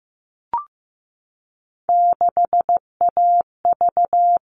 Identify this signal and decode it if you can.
Morse code, heard as a keyed tone, about 15 wpm, 700 Hz. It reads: 6AV
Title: All Cut Numbers - 3-Digit 15wpm